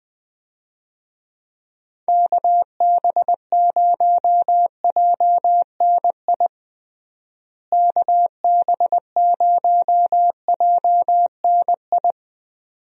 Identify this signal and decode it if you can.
Morse code, heard as a keyed tone, about 20 wpm, 700 Hz. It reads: KB0JNI KB0JNI
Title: USA Call Signs - 6 Characters - Repeated 2x - 20wpm